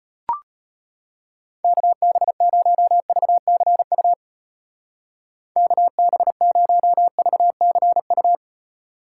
Morse code, heard as a keyed tone, about 30 wpm, 700 Hz. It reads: KB0VCU KB0VCU
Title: Speed Racing - USA 2x3 Call Signs 25wpm